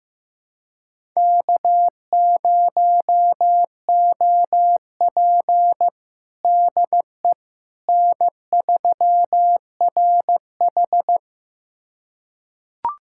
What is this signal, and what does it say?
Morse code, heard as a keyed tone, about 15 wpm, 700 Hz. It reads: K0OP DE N3RH
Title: USA Callsign DE Callsign - 4 Characters - 15wpm